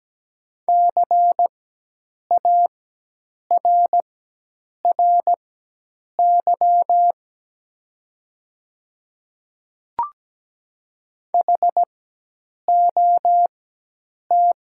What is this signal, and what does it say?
Morse code, heard as a keyed tone, about 17 wpm, spaced as for 8 wpm, 700 Hz. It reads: CARRY HOT
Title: 4x Character Spacing - Top 500 Words - 17wpm